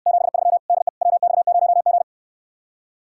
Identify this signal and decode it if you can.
Morse code, heard as a keyed tone, about 69 wpm, 700 Hz. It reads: 73 DE WX0G